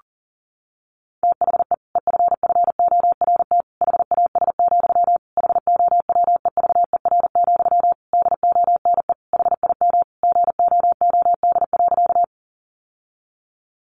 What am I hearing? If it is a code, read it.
Morse code, heard as a keyed tone, about 40 wpm, 700 Hz. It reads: THE EFFORT HAS, HOWEVER, DONE HIM GOOD.
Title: Sentences from Top 1200 Words 40wpm